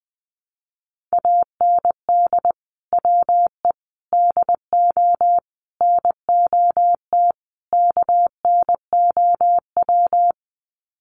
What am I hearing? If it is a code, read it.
Morse code, heard as a keyed tone, about 20 wpm, 700 Hz. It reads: AND WE DO NOT KNOW
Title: Sets of 5 Words Encore 20wpm